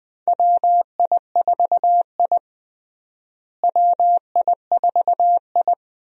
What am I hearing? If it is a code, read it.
Morse code, heard as a keyed tone, about 20 wpm, 700 Hz. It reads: WI4I WI4I
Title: USA Call Signs - 4 Characters - Repeated 3x - 20wpm